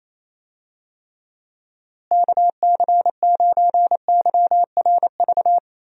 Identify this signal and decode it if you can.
Morse code, heard as a keyed tone, about 28 wpm, 700 Hz. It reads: KC9YRV
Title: USA Call Signs - 6 Characters - Repeated 2x - 28wpm